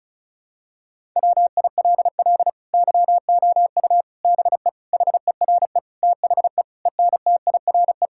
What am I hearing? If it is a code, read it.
Morse code, heard as a keyed tone, about 36 wpm, 700 Hz. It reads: WILL YOU BE HERE THE ENTIRE
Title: Sentences from Top 1000 Words - Review Part 5 - 35wpm